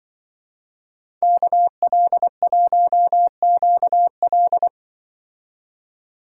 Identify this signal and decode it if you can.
Morse code, heard as a keyed tone, about 23 wpm, 700 Hz. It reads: KL1QL
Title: Speed Racing - USA 2x2 Call Signs 20wpm